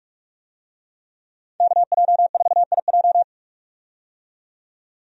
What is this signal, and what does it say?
Morse code, heard as a keyed tone, about 41 wpm, 700 Hz. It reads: KJ4IJ
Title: USA Call Signs - 5 Characters - 45wpm